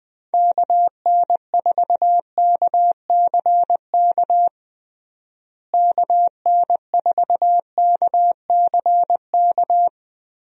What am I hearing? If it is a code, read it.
Morse code, heard as a keyed tone, about 21 wpm, 700 Hz. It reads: KN4KCK KN4KCK
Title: USA Call Signs - 6 Characters - Repeated 2x - 20wpm